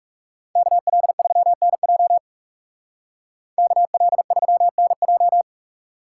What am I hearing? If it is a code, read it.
Morse code, heard as a keyed tone, about 39 wpm, 700 Hz. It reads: KL3NJ KL3NJ